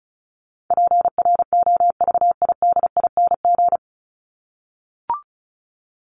Morse code, heard as a keyed tone, about 34 wpm, 700 Hz. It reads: PROVIDING
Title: ING Suffix - Top 100 Words with Base Words 35wpm